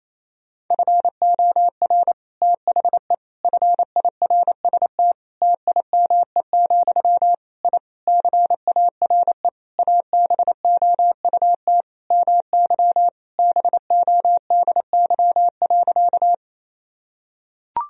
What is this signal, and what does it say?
Morse code, heard as a keyed tone, about 28 wpm, 700 Hz. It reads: FOR THE FIRST TIME, I CARE ABOUT MY BODY.
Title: Sentences from Top 1000 Words - Review Part 2 - 28wpm